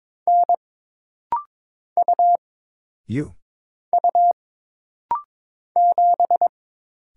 Morse code, heard as a keyed tone, about 22 wpm, 700 Hz. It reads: N U U 7
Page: This rapid-fire timing is geared to push your ICR (Instant Character Recognition) skills.